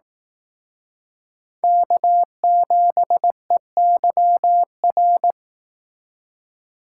Morse code, heard as a keyed tone, about 18 wpm, 700 Hz. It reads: K7EYR
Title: Speed Racing - USA Call Signs 15wpm